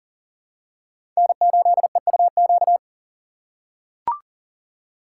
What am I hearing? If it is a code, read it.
Morse code, heard as a keyed tone, about 40 wpm, 700 Hz. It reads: N8EUQ
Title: USA Call Signs - 5 Characters - 4x Character Spacing - 40wpm